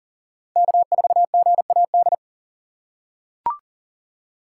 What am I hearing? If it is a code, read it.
Morse code, heard as a keyed tone, about 40 wpm, 700 Hz. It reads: K4GAD
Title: USA Call Signs - 5 Characters - 3x Character Spacing - 40wpm